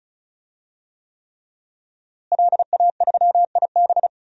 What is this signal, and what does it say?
Morse code, heard as a keyed tone, about 36 wpm, 700 Hz. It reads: LA3IB
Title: DXCC-100 Call Signs - 5 Characters - Repeated 3x - 35wpm